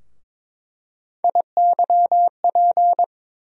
Morse code, heard as a keyed tone, about 23 wpm, 700 Hz. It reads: IYP
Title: USA Call Sign Postfixes - 3 Letters 22wpm